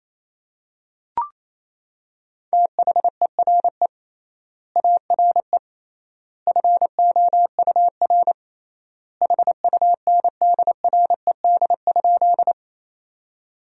Morse code, heard as a keyed tone, about 28 wpm, 700 Hz. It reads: THERE ARE FOUR HUNDRED?
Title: 3x Word-Spacing - Sentences from Top 400 Words - 28wpm